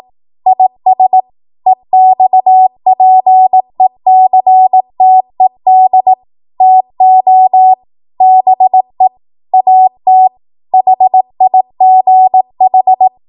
Morse code, heard as a keyed tone, about 18 wpm, 750 Hz. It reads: IS EXPECTED TO BE AT HIGH
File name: Morse / 18WPM